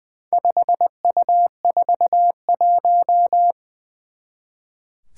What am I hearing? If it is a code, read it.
Morse code, heard as a keyed tone, about 20 wpm, 700 Hz. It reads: 5U41
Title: Four Challenging Letter-Number Combinations 20wpm